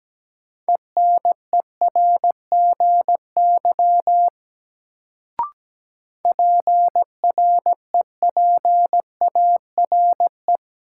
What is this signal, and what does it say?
Morse code, heard as a keyed tone, about 17 wpm, 700 Hz. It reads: ENERGY PREPARE
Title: Sentences from Top 900 Words - New Word Drill - 17wpm